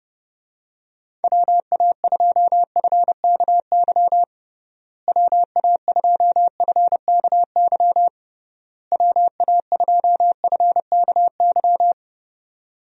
Morse code, heard as a keyed tone, about 30 wpm, 700 Hz. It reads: WA2FKY WA2FKY WA2FKY
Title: USA Call Signs - 6 Characters - Repeated 3x - 30wpm